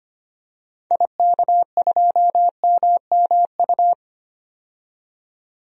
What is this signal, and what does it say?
Morse code, heard as a keyed tone, about 25 wpm, 700 Hz. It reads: IK2MMU
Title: DXCC-100 Call Signs - 6 Characters - 25wpm